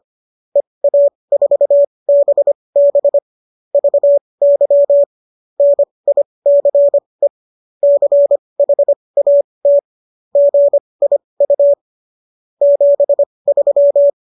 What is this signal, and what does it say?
Morse code, heard as a keyed tone, about 25 wpm, spaced as for 20 wpm, 570 Hz. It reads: EA4BB VY NICE CHAT GIU 73